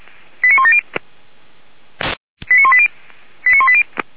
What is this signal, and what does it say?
Five-tone selective call (ZVEI), heard as quick runs of five tones, tones of about 70 ms. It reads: 89170 89180 89180